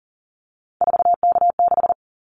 Morse code, heard as a keyed tone, about 40 wpm, 690 Hz. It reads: 4K6
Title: Callsign Prefixes for Top 100 DXCC - 40wpm